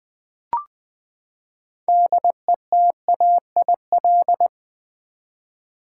Morse code, heard as a keyed tone, about 21 wpm, 700 Hz. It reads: DETAIL
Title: Six Letter Words - 20wpm